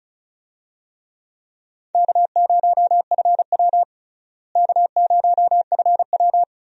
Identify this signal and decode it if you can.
Morse code, heard as a keyed tone, about 36 wpm, 700 Hz. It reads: K0FW K0FW
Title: USA Call Signs - 4 Characters - Repeated 3x - 35wpm